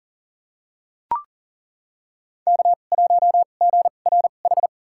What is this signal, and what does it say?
Morse code, heard as a keyed tone, about 40 wpm, spaced as for 29 wpm, 700 Hz. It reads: K1GRH
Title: USA Call Signs - 5 Characters - 2x Character Spacing - 40wpm